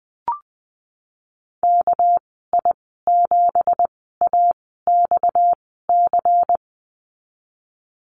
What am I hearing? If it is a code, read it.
Morse code, heard as a keyed tone, about 19 wpm, spaced as for 14 wpm, 700 Hz. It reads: KI7AXC
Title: 231 Lesson 37g - Letter x - Call Signs 20wpm